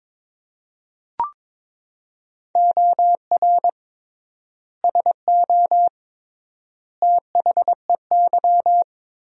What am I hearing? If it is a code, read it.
Morse code, heard as a keyed tone, about 22 wpm, 700 Hz. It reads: OR SO THEY
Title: 3x Word-Spacing - Sentences from Top 100 Words - 22wpm